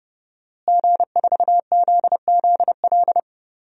Morse code, heard as a keyed tone, about 31 wpm, 700 Hz. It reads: G4ZZL
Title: DXCC-100 Call Signs - 5 Characters - Repeated 3x - 30wpm